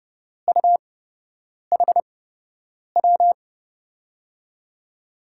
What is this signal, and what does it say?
Morse code, heard as a keyed tone, about 30 wpm, spaced as for 8 wpm, 700 Hz. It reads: UHW
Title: USA Call Signs - 6 Characters - 8x Character Spacing - 30wpm